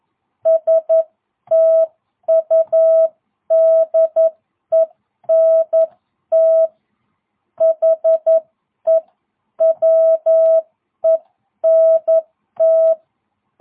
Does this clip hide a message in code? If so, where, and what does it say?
Yes: Morse code from 0.5 to 12.9 s, heard as a keyed tone, about 11 wpm, spaced as for 9 wpm, 650 Hz. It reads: STUDENT HE WENT